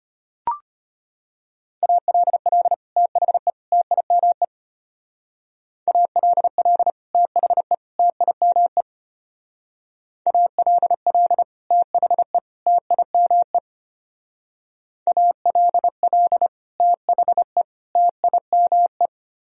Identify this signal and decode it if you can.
Morse code, heard as a keyed tone, about 32 wpm, 700 Hz. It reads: ALL THE TIME ALL THE TIME ALL THE TIME ALL THE TIME